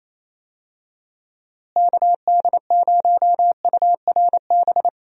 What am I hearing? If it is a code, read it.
Morse code, heard as a keyed tone, about 28 wpm, 700 Hz. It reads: KD0URB
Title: USA Call Signs - 6 Characters - Repeated 3x - 28wpm